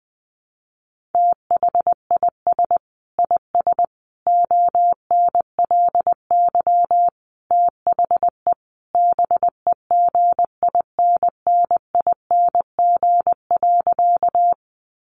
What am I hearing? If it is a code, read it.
Morse code, heard as a keyed tone, about 20 wpm, 700 Hz. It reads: THIS IS ONLY THE BEGINNING.
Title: Sentences from Top 800 Words 20wpm